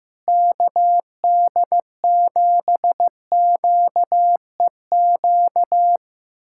Morse code, heard as a keyed tone, about 16 wpm, 700 Hz. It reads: KD7QEQ
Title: USA Callsign DE Callsign 15wpm